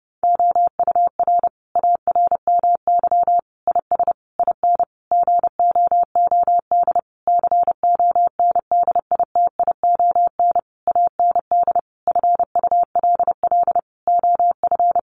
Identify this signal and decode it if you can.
Morse code, heard as a keyed tone, about 30 wpm, 700 Hz. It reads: OUR ARMY IS IN GOOD CONDITION AND FULL OF
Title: Sentences from Top 1500 Words 30wpm